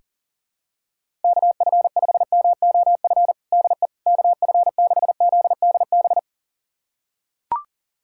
Morse code, heard as a keyed tone, about 40 wpm, 700 Hz. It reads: KF5MOF DE KF6ZDB
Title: USA Callsign DE Callsign - 6 Characters - 40wpm